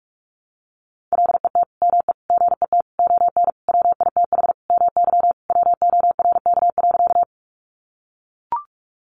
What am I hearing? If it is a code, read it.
Morse code, heard as a keyed tone, about 44 wpm, 700 Hz. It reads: LET ME GET ON WITH MY WORK.
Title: Sentences from Top 300 Words 45wpm